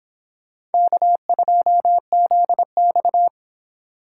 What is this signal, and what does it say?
Morse code, heard as a keyed tone, about 29 wpm, 700 Hz. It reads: K2ZX